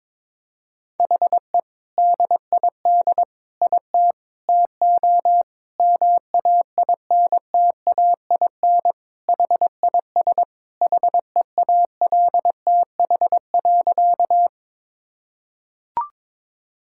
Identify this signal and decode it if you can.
Morse code, heard as a keyed tone, about 22 wpm, 700 Hz. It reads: HE DID IT TO MAINTAIN HIS HEALTH.
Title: Sentences from Top 1000 Words - Review Part 5 - 22wpm